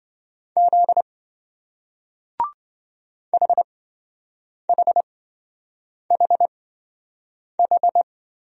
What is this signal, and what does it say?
Morse code, heard as a keyed tone, about 27 wpm, spaced as for 7 wpm, 700 Hz. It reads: Z HHHH